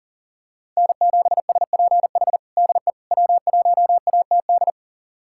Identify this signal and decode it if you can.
Morse code, heard as a keyed tone, about 40 wpm, 700 Hz. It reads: N7SPH DE W1ATD
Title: USA Callsign DE Callsign - 5 Characters - 40wpm